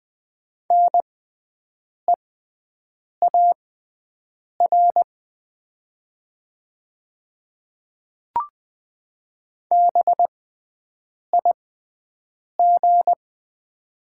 Morse code, heard as a keyed tone, about 20 wpm, spaced as for 7 wpm, 700 Hz. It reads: NEAR BIG
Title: 6x Character Spacing - Top 300 Words - 20wpm